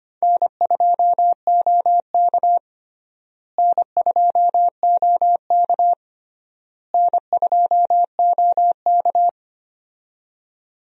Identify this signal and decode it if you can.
Morse code, heard as a keyed tone, about 25 wpm, 700 Hz. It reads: N2OK N2OK N2OK
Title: USA Call Signs - 4 Characters - Repeated 3x - 25wpm